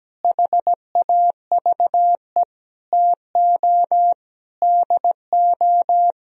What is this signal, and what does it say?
Morse code, heard as a keyed tone, about 17 wpm, 700 Hz. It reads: HAVE TO DO
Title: Sentences from Top 1000 Words - Review Part 3 - 17wpm